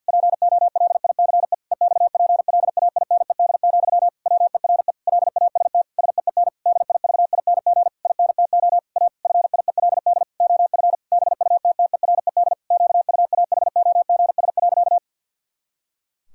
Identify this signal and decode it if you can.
Morse code, heard as a keyed tone, about 52 wpm, 700 Hz. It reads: POLICE EXPLAINED, WERE LAST SEEN DIVING INTO A FIELD OF BUTTERED QUAHOGS.
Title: Pangram Challenge v2 - 50wpm